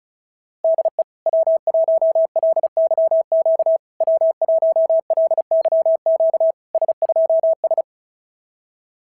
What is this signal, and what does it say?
Morse code, heard as a keyed tone, about 38 wpm, 650 Hz. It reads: DE W1LYQ W1LYQ S2S
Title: SOTA QSO Practice 35wpm